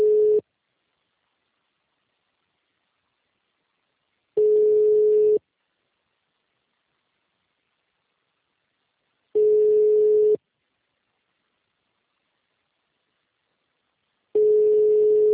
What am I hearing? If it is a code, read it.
Ringback tone (Europe, 425 Hz).
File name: out- -2004- - - .269